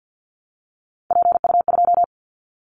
Morse code, heard as a keyed tone, about 50 wpm, 700 Hz. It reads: PU2
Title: Callsign Prefixes for Top 50 DXCC - 50wpm